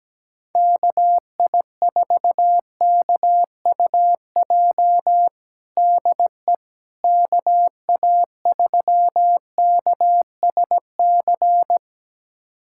Morse code, heard as a keyed tone, about 17 wpm, 700 Hz. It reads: KI4KUJ DE KA3KSC
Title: USA Callsign DE Callsign 17wpm